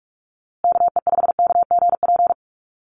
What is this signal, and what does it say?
Morse code, heard as a keyed tone, about 44 wpm, 700 Hz. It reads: KE5KGP
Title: USA Call Signs Repeated 3x 45wpm